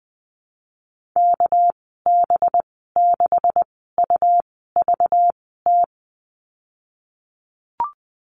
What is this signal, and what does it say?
Morse code, heard as a keyed tone, about 20 wpm, spaced as for 15 wpm, 700 Hz. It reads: KB6UVT
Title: 219 Lesson 35g - Number 0 - Call Signs 20wpm